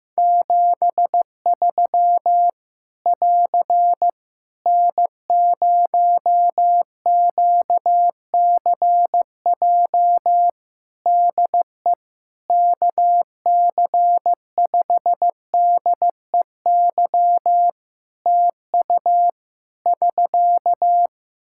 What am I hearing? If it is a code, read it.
Morse code, heard as a keyed tone, about 15 wpm, 700 Hz. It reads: 73 <AR> N0QCJ DE KC5DEY TU <SK>
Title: Practice QSO - Exchange 4 - Ending - 15wpm